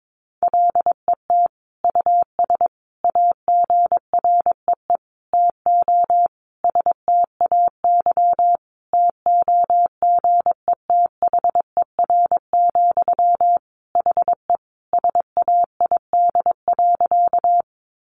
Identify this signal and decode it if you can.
Morse code, heard as a keyed tone, about 22 wpm, 700 Hz. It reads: LET US AGREE TO STAY TOGETHER, HE SAID.
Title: Sentences from Top 1100 Words 22wpm